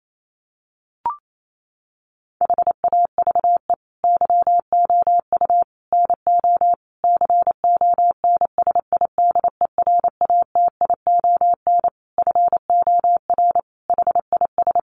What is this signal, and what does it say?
Morse code, heard as a keyed tone, about 28 wpm, 700 Hz. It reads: HAVE YOU NO CONSIDERATION FOR HIS
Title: Sentences from Top 1500 Words 28wpm